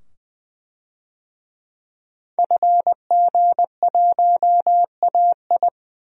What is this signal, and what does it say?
Morse code, heard as a keyed tone, about 20 wpm, 700 Hz. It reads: FG1AI
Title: DXCC-100 Call Signs - 5 Characters - 20wpm